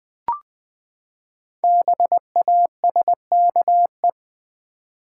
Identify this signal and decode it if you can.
Morse code, heard as a keyed tone, about 20 wpm, 700 Hz. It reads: BASKE
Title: Build A Word - 5 Letters At A Time - 10-Letter Words - 20wpm